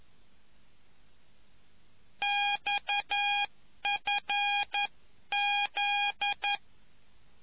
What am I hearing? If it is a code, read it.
Morse code, heard as a keyed tone, about 11 wpm, 820 Hz. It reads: XFZ